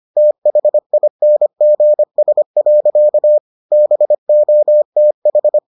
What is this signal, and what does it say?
Morse code, heard as a keyed tone, about 25 wpm, 600 Hz. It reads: THINGS. BOTH